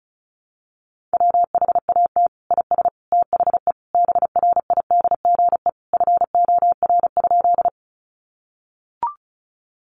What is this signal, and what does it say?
Morse code, heard as a keyed tone, about 34 wpm, 700 Hz. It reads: WHAT IS THE BRIDGE FOR?
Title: Sentences from Top 1200 Words 35wpm